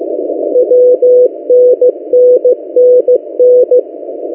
Morse code, heard as a keyed tone, about 15 wpm, 500 Hz. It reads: WNNNN
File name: M08a- -1355utc-8096khz